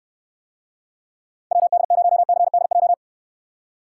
Morse code, heard as a keyed tone, about 68 wpm, 700 Hz. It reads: WD0CDJ